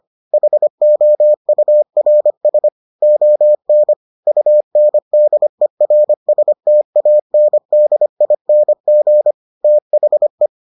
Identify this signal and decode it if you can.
Morse code, heard as a keyed tone, about 25 wpm, 600 Hz. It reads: HOURS ON UNDERSTANDING THE